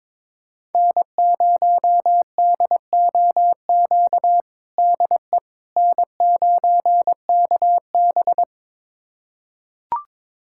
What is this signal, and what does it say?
Morse code, heard as a keyed tone, about 22 wpm, 700 Hz. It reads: N0DOQ DE N9KB
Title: USA Callsign DE Callsign 22wpm